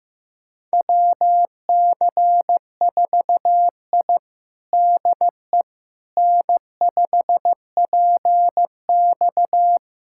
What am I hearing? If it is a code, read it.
Morse code, heard as a keyed tone, about 15 wpm, 700 Hz. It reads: WC4I DE N5PX
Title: USA Callsign DE Callsign - 4 Characters - 15wpm